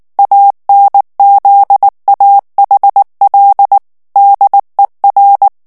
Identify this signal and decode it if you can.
Morse code, heard as a keyed tone, about 19 wpm, 800 Hz. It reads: ANZAHL DER
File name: morse mp3 / ALLEMAND texte clair 2 vitesse 18 mots minute